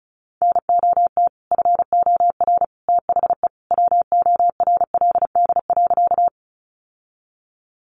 Morse code, heard as a keyed tone, about 34 wpm, 700 Hz. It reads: NOT FOR THE WORLD.
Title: Sentences from Top 600 Words 35wpm